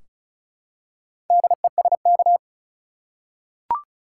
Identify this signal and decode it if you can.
Morse code, heard as a keyed tone, about 36 wpm, 700 Hz. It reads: DESK
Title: Four Letter Words - 35wpm